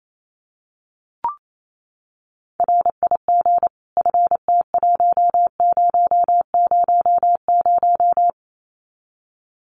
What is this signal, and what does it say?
Morse code, heard as a keyed tone, about 28 wpm, 700 Hz. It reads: RIG FT1000
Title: Common QSO Phrases 28wpm